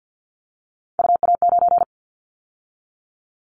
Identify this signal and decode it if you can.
Morse code, heard as a keyed tone, about 50 wpm, 700 Hz. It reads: UA9
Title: Callsign Prefixes for Top 25 DXCC - 50wpm